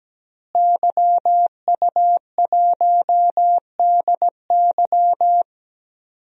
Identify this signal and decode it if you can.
Morse code, heard as a keyed tone, about 17 wpm, 700 Hz. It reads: YU1DY
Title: DXCC-100 Call Signs - 5 Characters - Repeated 3x - 17wpm